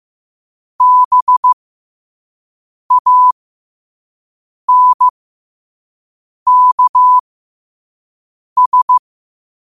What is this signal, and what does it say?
Morse code, heard as a keyed tone, about 15 wpm, spaced as for 5 wpm, 1000 Hz. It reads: BANKS
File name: qotdfiles / 5wpm